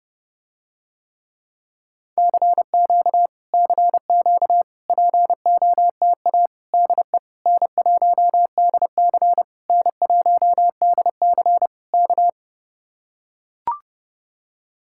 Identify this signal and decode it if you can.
Morse code, heard as a keyed tone, about 30 wpm, 700 Hz. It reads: CQ CQ POTA DE N1DC N1DC K